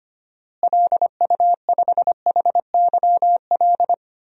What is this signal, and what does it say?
Morse code, heard as a keyed tone, about 25 wpm, 700 Hz. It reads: LU5HYL
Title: DXCC-100 Call Signs - 6 Characters - Repeated 2x - 25wpm